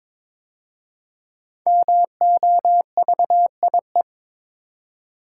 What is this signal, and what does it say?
Morse code, heard as a keyed tone, about 23 wpm, 700 Hz. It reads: MOVIE
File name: 5 Letter Words - First 3 Letters Given - 22wpm